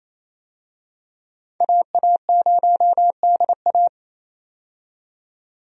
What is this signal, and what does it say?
Morse code, heard as a keyed tone, about 28 wpm, 700 Hz. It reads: AA0DA
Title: USA Call Signs - 5 Characters - 28wpm